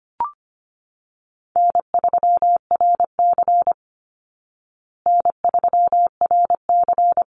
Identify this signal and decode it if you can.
Morse code, heard as a keyed tone, about 25 wpm, 700 Hz. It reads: N3RC N3RC
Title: Call Signs Repeated 3x - 25wpm